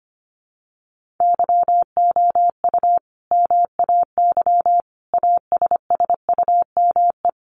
Morse code, heard as a keyed tone, about 25 wpm, 700 Hz. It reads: YOU MAY ASSUME
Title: Sentences from Top 1500 Words 25wpm